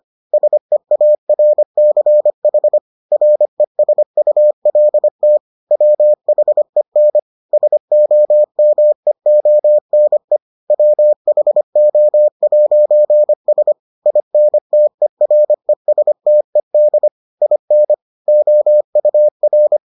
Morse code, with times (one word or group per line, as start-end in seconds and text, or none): 0.3-2.8 SEARCH
3.1-5.4 RESULT
5.7-7.2 WHEN
7.5-10.4 SOMEONE
10.7-13.7 WHO'S
14.1-17.1 INTERESTED
17.4-17.9 IN
18.3-19.8 OUR